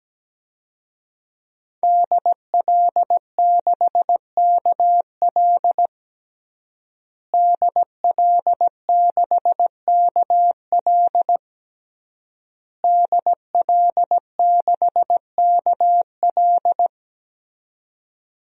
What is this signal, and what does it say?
Morse code, heard as a keyed tone, about 17 wpm, 700 Hz. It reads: DL6KL DL6KL DL6KL
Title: DXCC-100 Call Signs - 5 Characters - Repeated 3x - 17wpm